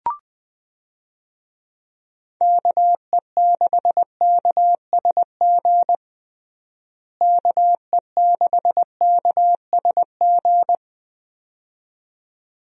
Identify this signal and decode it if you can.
Morse code, heard as a keyed tone, about 20 wpm, 700 Hz. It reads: KE6KSG KE6KSG
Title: USA Call Signs - 6 Characters - Repeated 2x - 20wpm